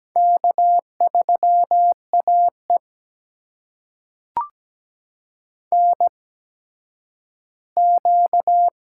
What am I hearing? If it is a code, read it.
Morse code, heard as a keyed tone, about 17 wpm, spaced as for 10 wpm, 700 Hz. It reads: K3AE NQ
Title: USA Call Signs - 8x Character Spacing - 17wpm